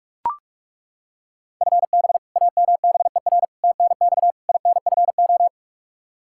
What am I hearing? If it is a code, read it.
Morse code, heard as a keyed tone, about 41 wpm, 700 Hz. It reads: FB AMBER TNX INFO
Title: Practice QSO - Exchange 3 - More Info - 45wpm